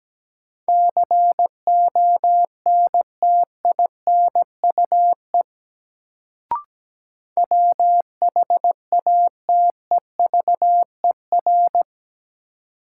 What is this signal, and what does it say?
Morse code, heard as a keyed tone, about 17 wpm, 700 Hz. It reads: CONTINUE WHATEVER
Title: Sentences from Top 1100 Words - New Word Drill Alt - 17wpm